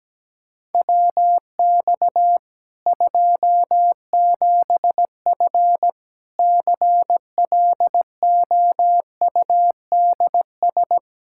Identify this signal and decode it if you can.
Morse code, heard as a keyed tone, about 17 wpm, 700 Hz. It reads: WX 27F CLOUDS
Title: K1USN SST QSO Practice 17wpm